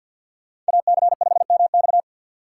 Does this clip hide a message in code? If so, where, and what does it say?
Yes: Morse code from 0.7 to 2.0 s, heard as a keyed tone, about 63 wpm, 700 Hz. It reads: AC5MX